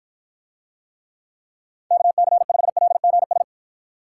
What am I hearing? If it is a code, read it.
Morse code, heard as a keyed tone, about 53 wpm, 700 Hz. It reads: KC5LGS